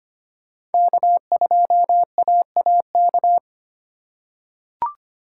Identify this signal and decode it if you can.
Morse code, heard as a keyed tone, about 25 wpm, 700 Hz. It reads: K2AAK
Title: USA Call Signs - 5 Characters - 2x Character Spacing - 25wpm